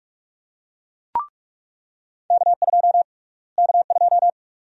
Morse code, heard as a keyed tone, about 42 wpm, 700 Hz. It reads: K2 K2
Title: USA Call Sign Prefixes - Repeated 2x - 45wpm